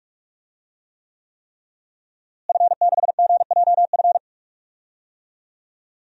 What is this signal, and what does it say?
Morse code, heard as a keyed tone, about 41 wpm, 700 Hz. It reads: F6GJF
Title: DXCC-100 Call Signs - 5 Characters - Repeated 3x - 45wpm